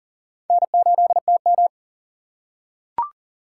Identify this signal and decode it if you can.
Morse code, heard as a keyed tone, about 40 wpm, 700 Hz. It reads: N8TM
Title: USA Call Signs - 3x Character Spacing - 40wpm